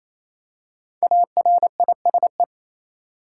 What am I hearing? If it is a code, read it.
Morse code, heard as a keyed tone, about 28 wpm, 700 Hz. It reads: ARISE
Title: Five Letter Words - 28wpm